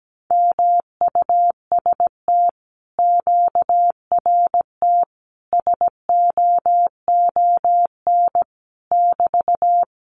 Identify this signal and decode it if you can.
Morse code, heard as a keyed tone, about 17 wpm, 700 Hz. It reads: MUST QRT SOON <BT>
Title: Common QSO Phrases Encore 17wpm